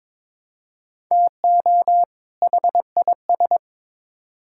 Morse code, heard as a keyed tone, about 21 wpm, 700 Hz. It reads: TO HIS
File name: Build A Phrase - 2-Word Phrases - 22wpm